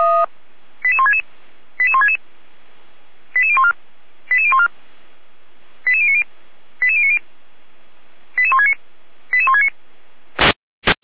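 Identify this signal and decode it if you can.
Five-tone selective call (ZVEI), heard as quick runs of five tones, tones of about 70 ms. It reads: 88166 88166 88014 88014 88090 88090 88169 88169
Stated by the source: ZVEI five-tone call received by SDR.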